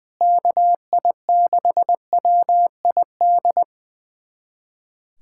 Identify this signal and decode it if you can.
Morse code, heard as a keyed tone, about 20 wpm, 700 Hz. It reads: KI6WID
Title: USA Callsign DE Callsign - 6 Characters - 20wpm